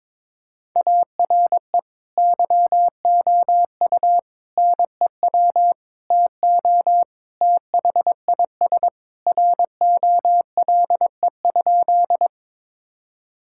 Morse code, heard as a keyed tone, about 21 wpm, 700 Hz. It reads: ARE YOU NEW TO THIS ROLE?
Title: Sentences from Top 1000 Words - Review Part 6 - 22wpm